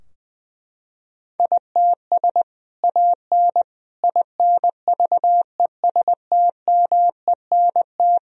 Morse code, heard as a keyed tone, about 20 wpm, 700 Hz. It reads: ITS AN INVESTMENT
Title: Sentences from Top 1000 Words - Review Part 4 - 20wpm